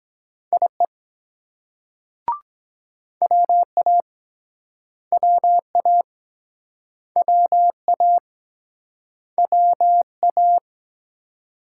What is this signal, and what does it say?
Morse code, heard as a keyed tone, about 22 wpm, 700 Hz. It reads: IE WA WA WA WA